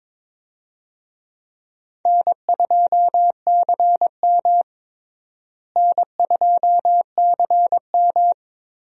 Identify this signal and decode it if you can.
Morse code, heard as a keyed tone, about 21 wpm, 700 Hz. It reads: N2CM N2CM
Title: USA Call Signs - 4 Characters - Repeated 3x - 22wpm